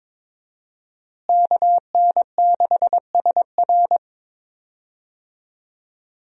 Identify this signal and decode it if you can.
Morse code, heard as a keyed tone, about 22 wpm, 700 Hz. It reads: KN6SR
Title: USA Call Signs - 5 Characters - Repeated 2x - 22wpm